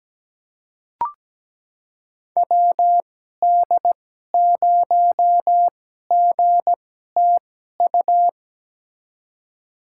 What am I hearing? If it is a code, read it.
Morse code, heard as a keyed tone, about 17 wpm, spaced as for 12 wpm, 700 Hz. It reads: WD0GTU
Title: USA Call Signs - 6 Characters - 2x Character Spacing - 17wpm